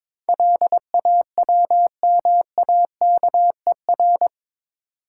Morse code, heard as a keyed tone, about 22 wpm, 700 Hz. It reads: LAWMAKER
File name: practice / 22wpm / Eight Letter Words - 22wpm